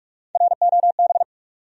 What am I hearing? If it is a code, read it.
Morse code, heard as a keyed tone, about 46 wpm, 700 Hz. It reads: ROB
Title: USA Common Names - Max 3 Letters - 45wpm